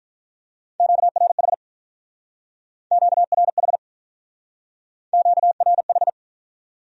Morse code, heard as a keyed tone, about 46 wpm, 700 Hz. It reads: QRH QRH QRH